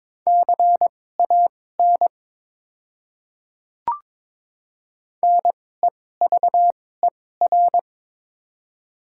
Morse code, heard as a keyed tone, about 21 wpm, spaced as for 16 wpm, 700 Hz. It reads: CAN NEVER
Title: 2x Character Spacing - Top 300 Words - 22wpm